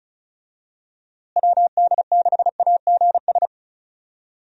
Morse code, heard as a keyed tone, about 34 wpm, 700 Hz. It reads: WD6AGS
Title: USA Call Signs - 6 Characters - Repeated 3x - 35wpm